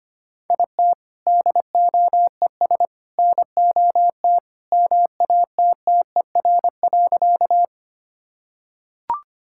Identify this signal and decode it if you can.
Morse code, heard as a keyed tone, about 25 wpm, 700 Hz. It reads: IT DOES NOT MATTER.
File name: practice / 25wpm / Sentences with 3-Word Phrases - 25wpm